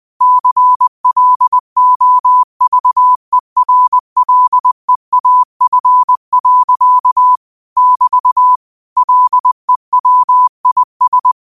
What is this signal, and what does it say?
Morse code, heard as a keyed tone, about 20 wpm, 1000 Hz. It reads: CLOVERLEAF. <BT> LEWIS